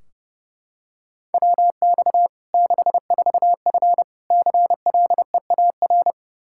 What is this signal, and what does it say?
Morse code, heard as a keyed tone, about 30 wpm, 700 Hz. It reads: WX 64F CLEAR
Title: Practice QSO - Exchange 2 - Info - 30wpm